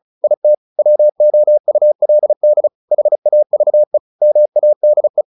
Morse code, heard as a keyed tone, about 34 wpm, 600 Hz. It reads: IT WOULD HAVE MADE
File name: the old man and the sea 2 35wpm